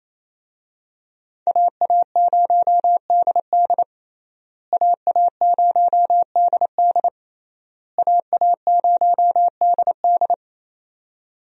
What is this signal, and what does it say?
Morse code, heard as a keyed tone, about 28 wpm, 700 Hz. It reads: AA0DD AA0DD AA0DD
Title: USA Call Signs - 5 Characters - Repeated 3x - 28wpm